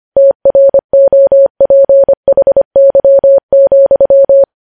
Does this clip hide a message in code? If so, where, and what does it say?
Yes: Morse code from 0.2 to 4.4 s, heard as a keyed tone, about 25 wpm, 550 Hz. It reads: TROPHY,